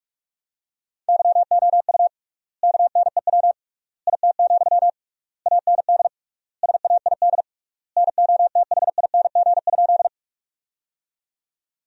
Morse code, heard as a keyed tone, about 46 wpm, 700 Hz. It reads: YOU KNEW IT, AND SAID NOTHING?
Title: 3x Word-Spacing - Sentences from Top 500 Words - 45wpm